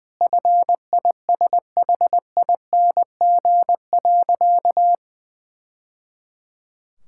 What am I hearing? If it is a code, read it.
Morse code, heard as a keyed tone, about 20 wpm, 700 Hz. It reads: FISHING.
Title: Sentences from Top 1000 Words - Review Part 5 - 20wpm